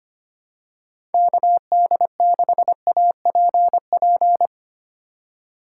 Morse code, heard as a keyed tone, about 25 wpm, 700 Hz. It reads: KD6APP
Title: USA Call Signs - 6 Characters - 2x Character Spacing - 25wpm